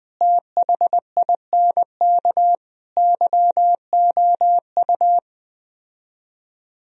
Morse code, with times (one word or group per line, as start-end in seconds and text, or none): 0.2-2.5 THINK
3.0-5.2 YOU